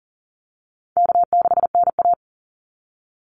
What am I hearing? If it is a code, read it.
Morse code, heard as a keyed tone, about 40 wpm, 700 Hz. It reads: K6NA
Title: Easy Call Signs Repeated 2x - 40wpm